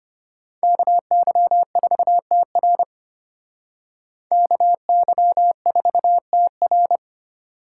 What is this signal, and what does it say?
Morse code, heard as a keyed tone, about 33 wpm, 700 Hz. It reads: KY4TR KY4TR